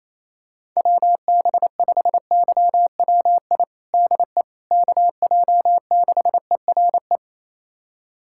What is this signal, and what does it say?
Morse code, heard as a keyed tone, about 28 wpm, 700 Hz. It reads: WB5YWI DE KJ6ERE
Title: USA Callsign DE Callsign - 6 Characters - 28wpm